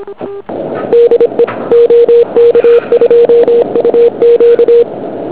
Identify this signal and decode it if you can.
Morse code, heard as a keyed tone, about 26 wpm, 470 Hz. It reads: DE OK2UQ
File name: ARCHIV / QRP2008 / OK1UN OK2UQ